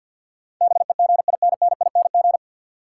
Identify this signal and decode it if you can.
Morse code, heard as a keyed tone, about 48 wpm, 700 Hz. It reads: BEGINNING
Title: Sentences from Top 800 Words - New Word Drill - 50wpm